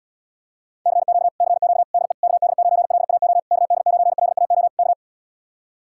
Morse code, heard as a keyed tone, about 80 wpm, 700 Hz. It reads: CQ CQ DE KD9LAO KD9LAO K